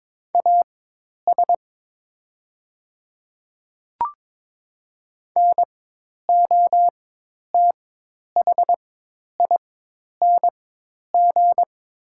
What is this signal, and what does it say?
Morse code, heard as a keyed tone, about 22 wpm, spaced as for 10 wpm, 690 Hz. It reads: AS NOTHING
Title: 4x Character Spacing - Top 500 Words - 22wpm